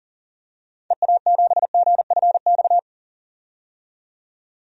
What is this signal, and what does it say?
Morse code, heard as a keyed tone, about 40 wpm, 700 Hz. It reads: EA7GFX
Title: DXCC-100 Call Signs - 6 Characters - 40wpm